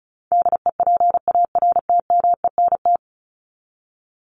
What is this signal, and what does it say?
Morse code, heard as a keyed tone, about 34 wpm, 700 Hz. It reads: DEPARTMENT
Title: Top 751-1000 Words 35wpm